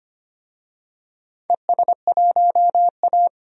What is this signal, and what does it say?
Morse code, heard as a keyed tone, about 25 wpm, 700 Hz. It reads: ES1A
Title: DXCC-100 Call Signs - 4 Characters - 25wpm